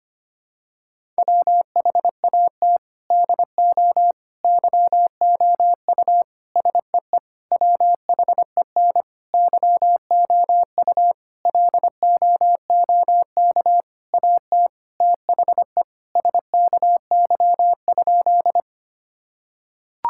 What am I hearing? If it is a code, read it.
Morse code, heard as a keyed tone, about 25 wpm, 700 Hz. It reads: WHAT DO YOU SEE WHEN YOU LOOK AT THE SKY?
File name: practice / 25wpm / Sentences with 4-Word Phrases - 25wpm